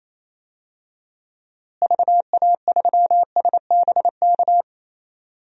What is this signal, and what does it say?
Morse code, heard as a keyed tone, about 27 wpm, 700 Hz. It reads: VA3SBK
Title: DXCC-100 Call Signs - 6 Characters - Repeated 2x - 28wpm